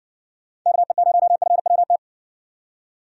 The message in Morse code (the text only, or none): KE0FWT